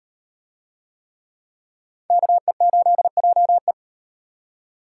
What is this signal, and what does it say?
Morse code, heard as a keyed tone, about 36 wpm, 700 Hz. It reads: KE8JE